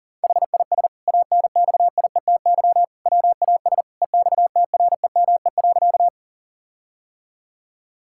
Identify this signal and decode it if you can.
Morse code, heard as a keyed tone, about 40 wpm, 700 Hz. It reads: HIS ANXIETY WAS EXTREME.
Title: Sentences from Top 1600 Words 40wpm